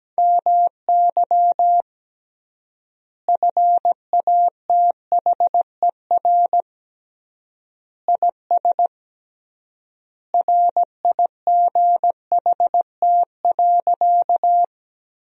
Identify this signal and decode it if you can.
Morse code, heard as a keyed tone, about 17 wpm, 700 Hz. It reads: MY FATHER IS RIGHT.
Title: 3x Word-Spacing - Sentences from Top 300 Words - 17wpm